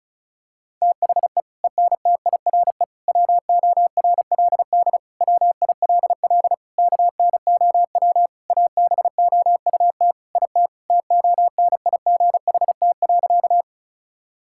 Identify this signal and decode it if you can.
Morse code, heard as a keyed tone, about 34 wpm, 700 Hz. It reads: THE ENTIRE WORLD WILL KNOW ABOUT IT TONIGHT.
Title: Sentences from Top 1000 Words - Review Part 5 - 35wpm